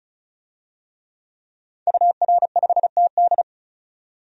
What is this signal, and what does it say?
Morse code, heard as a keyed tone, about 34 wpm, 700 Hz. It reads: UR5TD
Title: DXCC-100 Call Signs - 5 Characters - 35wpm